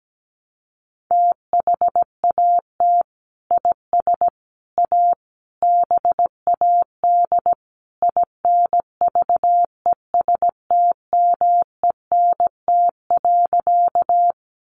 Morse code, heard as a keyed tone, about 17 wpm, 700 Hz. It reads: THAT IS A BAD INVESTMENT.
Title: Sentences from Top 700 Words 17wpm